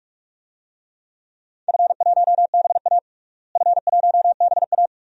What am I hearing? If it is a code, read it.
Morse code, heard as a keyed tone, about 41 wpm, 700 Hz. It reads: F1BA F1BA
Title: DXCC-100 Call Signs - 4 Characters - Repeated 2x - 45wpm